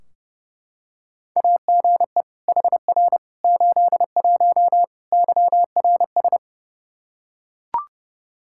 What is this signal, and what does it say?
Morse code, heard as a keyed tone, about 30 wpm, 700 Hz. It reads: AGE HR 81 YRS
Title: Practice QSO - Exchange 3 - More Info - 30wpm